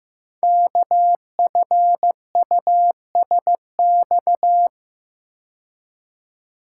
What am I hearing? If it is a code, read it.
Morse code, heard as a keyed tone, about 15 wpm, 700 Hz. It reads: KFUSX
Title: Code Groups - 5 Letters - 15wpm